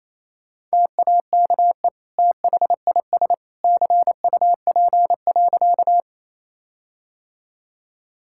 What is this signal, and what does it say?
Morse code, heard as a keyed tone, about 28 wpm, 700 Hz. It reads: TAKE THIS CUP.
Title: Sentences from Top 1600 Words 28wpm